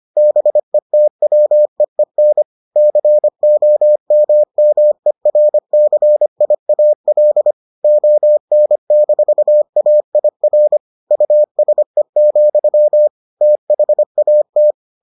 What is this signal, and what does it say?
Morse code, heard as a keyed tone, about 25 wpm, 600 Hz. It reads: BETWEEN COMMERCIAL ON-AIR USE, THAT